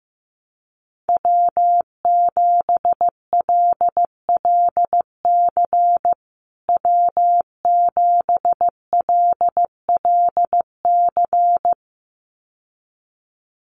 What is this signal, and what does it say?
Morse code, heard as a keyed tone, about 15 wpm, 700 Hz. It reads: W7LLC W7LLC
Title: USA Call Signs Repeated 2x 15wpm